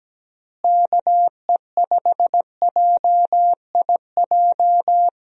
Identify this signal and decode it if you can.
Morse code, heard as a keyed tone, about 17 wpm, 700 Hz. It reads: KE5JIJ
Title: USA Call Signs - 6 Characters - Repeated 3x - 17wpm